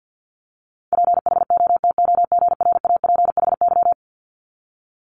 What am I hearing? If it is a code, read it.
Morse code, heard as a keyed tone, about 50 wpm, 700 Hz. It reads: PHOTOGRAPHY
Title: Multi-letter Phonemes 50wpm